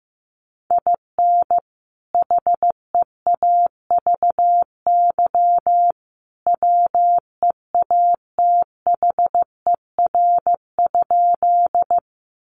Morse code, heard as a keyed tone, about 15 wpm, 700 Hz. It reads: IN HEAVY WEATHER?
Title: Sentences from Top 1100 Words 15wpm